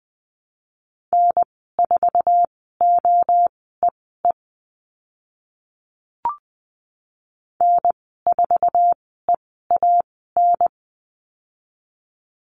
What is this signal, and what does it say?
Morse code, heard as a keyed tone, about 20 wpm, spaced as for 15 wpm, 700 Hz. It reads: N4OEE N4EAN
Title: 062 Lesson 11g - Letter h - Call Signs 20wpm